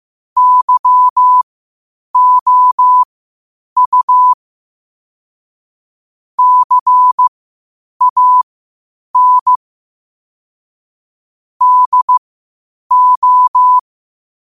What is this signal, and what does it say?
Morse code, heard as a keyed tone, about 15 wpm, spaced as for 8 wpm, 1000 Hz. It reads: YOU CAN DO
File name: qotdfiles / 8wpm